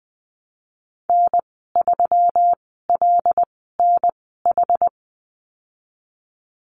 Morse code, heard as a keyed tone, about 20 wpm, spaced as for 15 wpm, 700 Hz. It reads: N3LNH
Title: 140 Lesson 23g - Letter f - Call Signs 20wpm